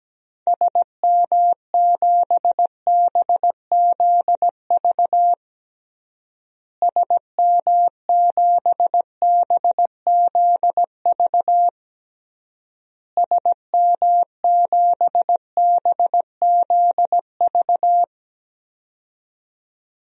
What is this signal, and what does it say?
Morse code, heard as a keyed tone, about 17 wpm, 700 Hz. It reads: SM7BZV SM7BZV SM7BZV
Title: DXCC-100 Call Signs - 6 Characters - Repeated 3x - 17wpm